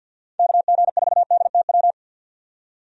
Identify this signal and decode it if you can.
Morse code, heard as a keyed tone, about 46 wpm, 700 Hz. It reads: KG4DTW
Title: USA Call Signs - 6 Characters - 50wpm